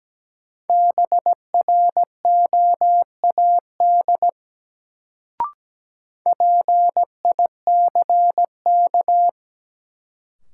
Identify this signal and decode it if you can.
Morse code, heard as a keyed tone, about 17 wpm, 700 Hz. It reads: BROAD PICK
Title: Sentences from Top 1000 Words - New Word Drill - 17wpm